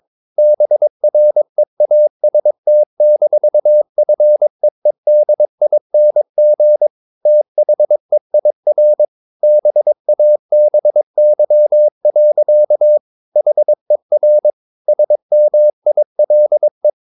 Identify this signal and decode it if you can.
Morse code, heard as a keyed tone, about 22 wpm, 600 Hz. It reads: BREAST-FEEDING THEIR BABY. HER SMILE
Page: Romanen Never af Ken Follet på engelsk.